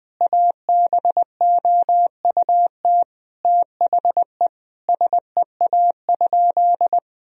Morse code, heard as a keyed tone, about 20 wpm, 700 Hz. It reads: ABOUT THE SEA?
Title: Sentences from Top 1000 Words - Review Part 1 - 20wpm